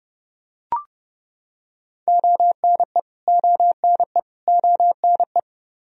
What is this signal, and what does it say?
Morse code, heard as a keyed tone, about 30 wpm, 700 Hz. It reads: ONE ONE ONE
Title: Top 100 Words Repeated 30wpm